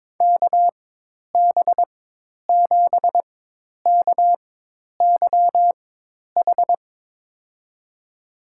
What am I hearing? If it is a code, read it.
Morse code, heard as a keyed tone, about 22 wpm, spaced as for 10 wpm, 700 Hz. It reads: KB7KYH
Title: USA Call Signs - 6 Characters - 4x Character Spacing - 22wpm